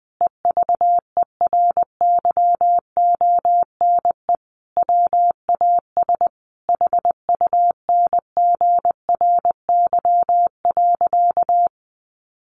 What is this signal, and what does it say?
Morse code, heard as a keyed tone, about 20 wpm, 700 Hz. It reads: EVERYONE WAS HUNGRY.
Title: Sentences from Top 1400 Words 20wpm